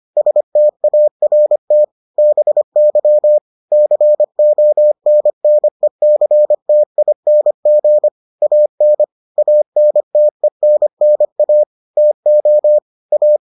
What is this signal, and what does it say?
Morse code, heard as a keyed tone, about 25 wpm, 600 Hz. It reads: START BY CONNECTING AN ANTENNA TO A